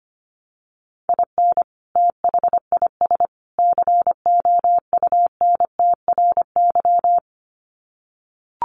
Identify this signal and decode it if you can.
Morse code, heard as a keyed tone, about 25 wpm, 700 Hz. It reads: IN THIS COUNTRY
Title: Sets of 3 Words 25wpm